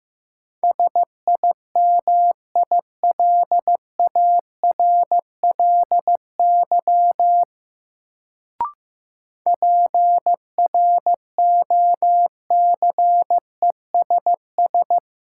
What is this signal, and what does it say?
Morse code, heard as a keyed tone, about 15 wpm, 700 Hz. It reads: SIMILARLY PROCESS
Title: Sentences from Top 1400 Words - New Word Drill Alt - 15wpm